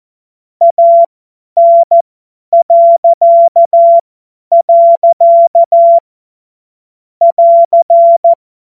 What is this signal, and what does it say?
Morse code, heard as a keyed tone, about 14 wpm, spaced as for 10 wpm, 680 Hz. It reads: AN.. <AR>
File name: Morsekurs / Witze / Witz 15. August 2023 50 F70